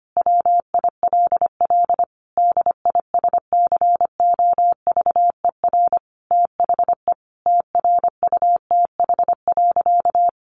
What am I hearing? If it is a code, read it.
Morse code, heard as a keyed tone, about 25 wpm, 700 Hz. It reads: WILL DISCOVER THE TRUTH.
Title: Sentences from Top 1100 Words 25wpm